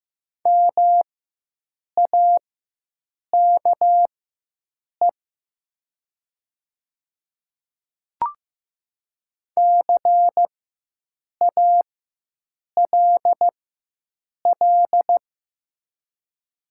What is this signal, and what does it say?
Morse code, heard as a keyed tone, about 15 wpm, spaced as for 7 wpm, 700 Hz. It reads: MAKE CALL
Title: 4x Character Spacing - Top 100 Words - 15wpm